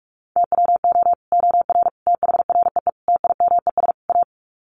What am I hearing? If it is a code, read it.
Morse code, heard as a keyed tone, about 44 wpm, 700 Hz. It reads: TWO OR THREE TIMES A
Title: Sets of 5 Words Encore2 45wpm